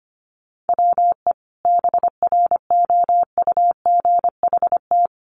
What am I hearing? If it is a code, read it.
Morse code, heard as a keyed tone, about 25 wpm, 700 Hz. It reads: WE BROUGHT
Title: Sentences from Top 500 Words 25wpm